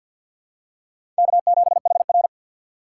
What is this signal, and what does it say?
Morse code, heard as a keyed tone, about 52 wpm, 700 Hz. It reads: K7HR
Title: USA Call Signs - 4 Characters - 3x Character Spacing - 50wpm